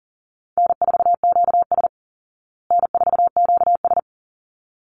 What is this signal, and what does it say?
Morse code, heard as a keyed tone, about 40 wpm, 700 Hz. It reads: N4QS N4QS
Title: Easy Call Signs Repeated 2x - 40wpm